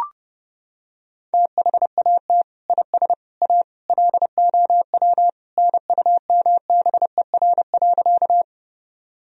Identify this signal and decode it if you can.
Morse code, heard as a keyed tone, about 32 wpm, 700 Hz. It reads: THAT IS A LOW NUMBER.
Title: Sentences from Top 1000 Words - Review Part 1 - 30wpm